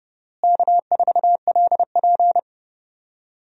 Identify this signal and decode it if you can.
Morse code, heard as a keyed tone, about 30 wpm, 700 Hz. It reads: K4LP
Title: USA Call Signs - 4 Characters - Repeated 2x - 30wpm